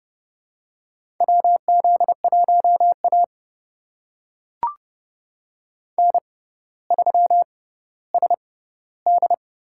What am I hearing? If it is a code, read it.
Morse code, heard as a keyed tone, about 30 wpm, spaced as for 15 wpm, 700 Hz. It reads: WZ1A N3SD
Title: USA Call Signs - 4 Characters - 6x Character Spacing - 30wpm